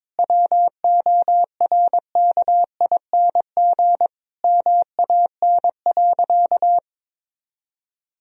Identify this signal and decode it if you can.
Morse code, heard as a keyed tone, about 22 wpm, 700 Hz. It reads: WORKING MAN.
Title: Sentences from Top 1000 Words - Review Part 4 - 22wpm